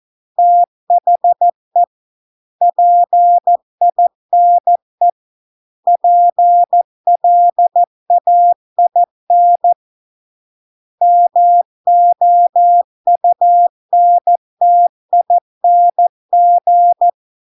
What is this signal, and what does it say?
Morse code, heard as a keyed tone, about 14 wpm, 700 Hz. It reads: THE PINE PLAIN MOUNTING